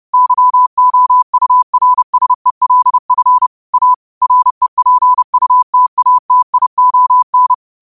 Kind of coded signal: Morse code